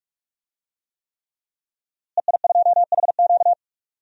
Morse code, heard as a keyed tone, about 41 wpm, 700 Hz. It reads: EI2HQ
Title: DXCC-100 Call Signs - 5 Characters - Repeated 3x - 45wpm